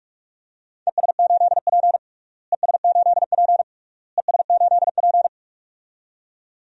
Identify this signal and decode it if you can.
Morse code, heard as a keyed tone, about 41 wpm, 700 Hz. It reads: ES8P ES8P ES8P
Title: DXCC-100 Call Signs - 4 Characters - Repeated 3x - 45wpm